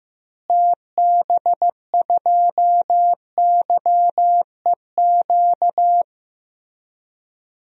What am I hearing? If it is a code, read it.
Morse code, heard as a keyed tone, about 15 wpm, 700 Hz. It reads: TB2YEQ
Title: DXCC-100 Call Signs - 6 Characters - Repeated 2x - 15wpm